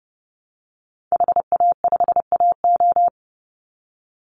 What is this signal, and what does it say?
Morse code, heard as a keyed tone, about 30 wpm, 700 Hz. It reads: HA5AO
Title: Call Signs Repeated 3x - 30wpm